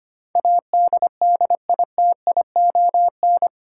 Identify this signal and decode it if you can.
Morse code, heard as a keyed tone, about 25 wpm, 700 Hz. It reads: ADDITION
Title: Sentences from Top 1500 Words - New Word Drill Alt - 25wpm